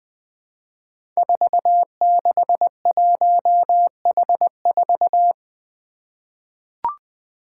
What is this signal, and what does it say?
Morse code, heard as a keyed tone, about 20 wpm, 700 Hz. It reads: 461H4
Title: Five Challenging Letter-Number Combinations 20wpm